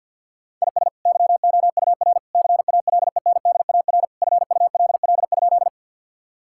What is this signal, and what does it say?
Morse code, heard as a keyed tone, about 50 wpm, 700 Hz. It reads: IS YOUR CALENDAR FULL?
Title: Sentences from Top 1600 Words 50wpm